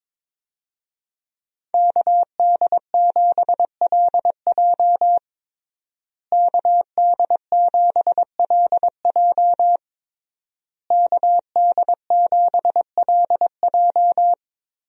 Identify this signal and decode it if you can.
Morse code, heard as a keyed tone, about 22 wpm, 700 Hz. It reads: KD7LJ KD7LJ KD7LJ
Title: USA Call Signs - 5 Characters - Repeated 3x - 22wpm